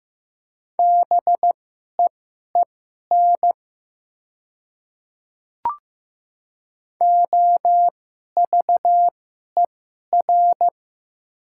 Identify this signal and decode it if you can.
Morse code, heard as a keyed tone, about 15 wpm, spaced as for 11 wpm, 700 Hz. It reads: BEEN OVER
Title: 2x Character Spacing - Top 100 Words - 15wpm